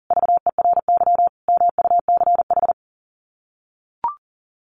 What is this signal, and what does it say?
Morse code, heard as a keyed tone, about 40 wpm, 700 Hz. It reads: VERY MUCH
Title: Sets of 5 Words 40wpm